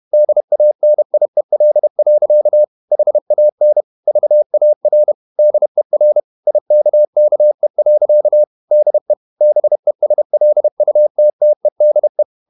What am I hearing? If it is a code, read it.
Morse code, heard as a keyed tone, about 31 wpm, 600 Hz. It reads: DANIEL. HAN VAR DER IKKE. DE BESLUTTEDE
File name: Ferie QRQ 31wpm